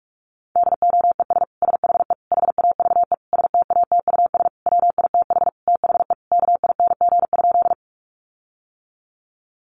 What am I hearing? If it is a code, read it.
Morse code, heard as a keyed tone, about 44 wpm, 700 Hz. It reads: DOES SHE HAVE STATUS WITH THE KING?
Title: Sentences from Top 1400 Words 45wpm